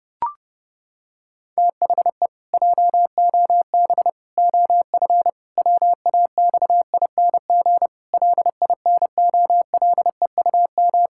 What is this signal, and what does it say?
Morse code, heard as a keyed tone, about 30 wpm, 700 Hz. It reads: THE JOB OF WAXING LINOLEUM
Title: Pangram Challenge v2 - 30wpm